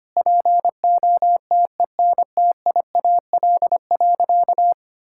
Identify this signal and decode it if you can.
Morse code, heard as a keyed tone, about 25 wpm, 700 Hz. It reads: POTENTIAL.
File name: practice / 25wpm / Sentences with 9-Word Phrases - 25wpm